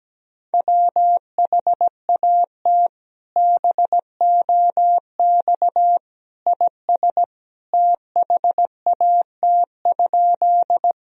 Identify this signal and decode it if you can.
Morse code, heard as a keyed tone, about 17 wpm, 700 Hz. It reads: WHAT BOX IS THAT?
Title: Sentences from Top 1000 Words - Review Part 2 - 17wpm